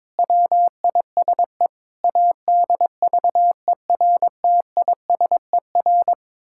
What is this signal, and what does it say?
Morse code, heard as a keyed tone, about 22 wpm, 700 Hz. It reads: WISE ADVERTISER
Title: Pangram Challenge v2 - 22wpm